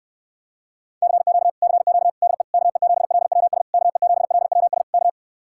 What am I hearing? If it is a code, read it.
Morse code, heard as a keyed tone, about 69 wpm, 700 Hz. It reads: CQ CQ DE KE7LWS KE7LWS K